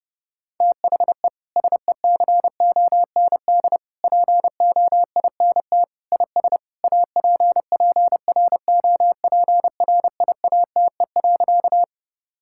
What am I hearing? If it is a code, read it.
Morse code, heard as a keyed tone, about 30 wpm, 700 Hz. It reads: THE SECOND POINT IS APPROPRIATE.
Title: Sentences from Top 1000 Words - Review Part 5 - 30wpm